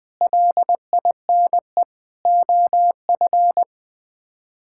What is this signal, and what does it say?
Morse code, heard as a keyed tone, about 20 wpm, 700 Hz. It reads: LINE OF
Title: Sets of 8 Words - 20wpm